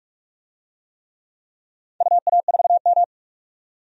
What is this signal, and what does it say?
Morse code, heard as a keyed tone, about 46 wpm, 700 Hz. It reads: UA4M
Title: DXCC-100 Call Signs - 4 Characters - 45wpm